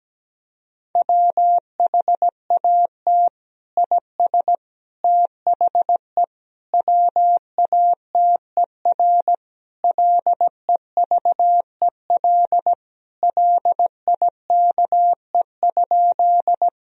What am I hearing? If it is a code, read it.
Morse code, heard as a keyed tone, about 17 wpm, 700 Hz. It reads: WHAT IS THE WATER LEVEL LIKE?
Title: Sentences from Top 1000 Words - Review Part 4 - 17wpm